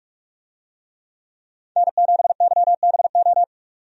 Morse code, heard as a keyed tone, about 46 wpm, 700 Hz. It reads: N7YBO
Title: USA Call Signs - 5 Characters - 45wpm